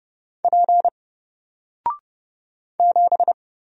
Morse code, heard as a keyed tone, about 30 wpm, 700 Hz. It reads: P 7
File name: Single-Letter-Number-30wpm